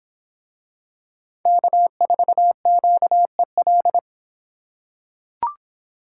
Morse code, heard as a keyed tone, about 26 wpm, 700 Hz. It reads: K4QEL